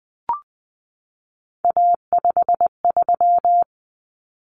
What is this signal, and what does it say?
Morse code, heard as a keyed tone, about 20 wpm, 700 Hz. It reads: A53
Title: Common Cut Numbers - 3-Digit 20wpm